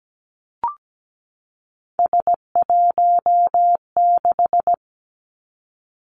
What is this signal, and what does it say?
Morse code, heard as a keyed tone, about 17 wpm, 700 Hz. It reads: S16
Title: Three Challenging Letter-Number Combinations 17wpm